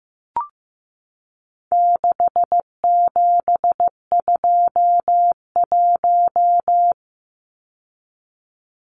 Morse code, heard as a keyed tone, about 15 wpm, 700 Hz. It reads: 6721
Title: Numbers - 4-Digit 15wpm